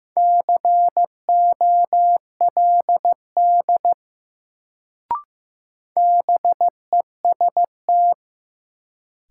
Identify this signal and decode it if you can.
Morse code, heard as a keyed tone, about 15 wpm, 700 Hz. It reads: COLD BEST
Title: Sentences from Top 400 Words - New Word Drill - 15wpm